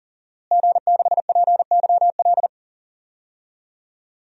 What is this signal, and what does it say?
Morse code, heard as a keyed tone, about 40 wpm, 700 Hz. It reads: G6PYL
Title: DXCC-100 Call Signs - 5 Characters - Repeated 2x - 40wpm